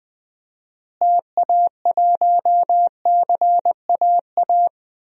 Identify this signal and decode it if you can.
Morse code, heard as a keyed tone, about 20 wpm, 700 Hz. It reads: TA1CAA
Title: DXCC-100 Call Signs - 6 Characters - Repeated 2x - 20wpm